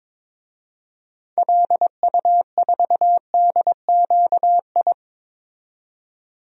Morse code, heard as a keyed tone, about 22 wpm, 700 Hz. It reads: LU4DQI
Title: DXCC-100 Call Signs - 6 Characters - 22wpm